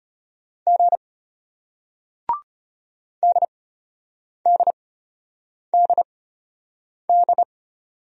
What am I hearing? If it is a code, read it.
Morse code, heard as a keyed tone, about 34 wpm, spaced as for 8 wpm, 700 Hz. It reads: G DDDD